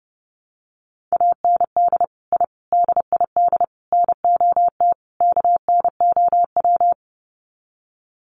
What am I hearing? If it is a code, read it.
Morse code, heard as a keyed tone, about 28 wpm, 700 Hz. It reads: AND I DID NOT KNOW
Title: Sets of 5 Words Encore 30wpm